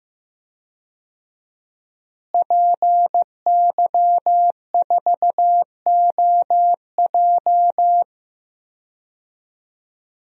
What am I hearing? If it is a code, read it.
Morse code, heard as a keyed tone, about 15 wpm, 700 Hz. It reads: PY4OJ
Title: DXCC-100 Call Signs - 5 Characters - Repeated 3x - 15wpm